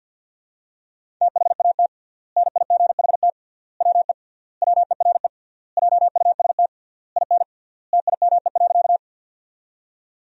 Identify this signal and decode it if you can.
Morse code, heard as a keyed tone, about 51 wpm, 700 Hz. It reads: THAT NIGHT WE WERE JUST IN TIME.
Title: 3x Word-Spacing - Sentences from Top 300 Words - 50wpm